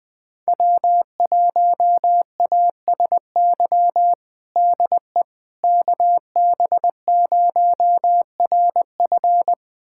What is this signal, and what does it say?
Morse code, heard as a keyed tone, about 20 wpm, 700 Hz. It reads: W1ASY DE KB0RF
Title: USA Callsign DE Callsign - 5 Characters - 20wpm